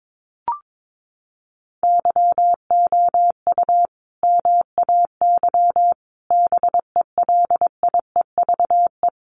Morse code, heard as a keyed tone, about 22 wpm, 700 Hz. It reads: YOU MAY BELIEVE
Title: Sentences from Top 1000 Words 22wpm